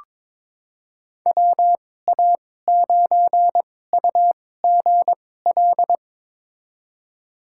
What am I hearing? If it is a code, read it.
Morse code, heard as a keyed tone, about 22 wpm, spaced as for 16 wpm, 700 Hz. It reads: WA9UGL
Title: USA Call Signs - 6 Characters - 2x Character Spacing - 22wpm